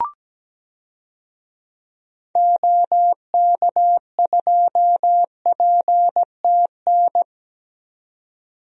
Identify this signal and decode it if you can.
Morse code, heard as a keyed tone, about 17 wpm, 700 Hz. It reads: OK2PTN
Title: DXCC-100 Call Signs - 6 Characters - 17wpm